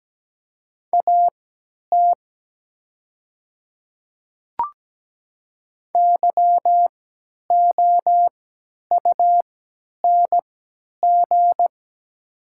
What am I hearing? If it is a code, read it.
Morse code, heard as a keyed tone, about 17 wpm, spaced as for 10 wpm, 700 Hz. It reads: AT YOUNG
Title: 3x Character Spacing - Top 500 Words - 17wpm